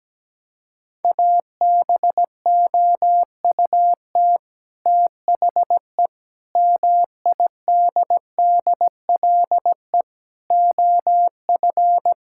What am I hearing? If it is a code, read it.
Morse code, heard as a keyed tone, about 17 wpm, 700 Hz. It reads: ABOUT THE MIDDLE OF
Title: Sentences from Top 1000 Words - Review Part 6 - 17wpm